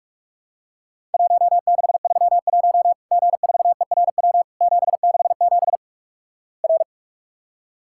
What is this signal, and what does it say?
Morse code, heard as a keyed tone, about 46 wpm, 700 Hz. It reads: 1631 G4ERW 767 R
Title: Worked All Europe DX Contest - QTC Exchange - 45wpm